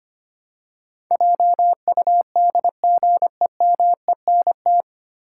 Morse code, heard as a keyed tone, about 25 wpm, 700 Hz. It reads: JUDGEMENT
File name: Nine Letter Words - 25wpm